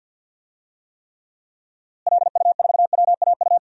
Morse code, heard as a keyed tone, about 50 wpm, 700 Hz. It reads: LU4WAU
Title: DXCC-100 Call Signs - 6 Characters - Repeated 3x - 50wpm